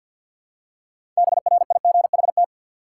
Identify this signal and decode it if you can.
Morse code, heard as a keyed tone, about 52 wpm, 700 Hz. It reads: BRIGHT
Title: Six Letter Words - 50wpm